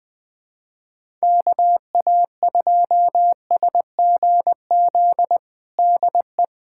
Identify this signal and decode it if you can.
Morse code, heard as a keyed tone, about 20 wpm, 700 Hz. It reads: KA2SGZ DE
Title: USA Callsign DE Callsign 20wpm